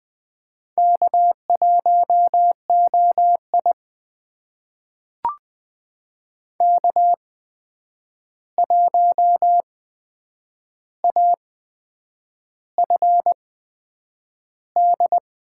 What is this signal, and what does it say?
Morse code, heard as a keyed tone, about 20 wpm, spaced as for 8 wpm, 700 Hz. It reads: K1OI K1AFD
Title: USA Call Signs - 8x Character Spacing - 20wpm